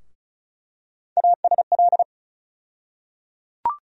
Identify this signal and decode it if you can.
Morse code, heard as a keyed tone, about 36 wpm, 700 Hz. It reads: ASL
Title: USA Call Sign Postfixes - 3 Letters 35wpm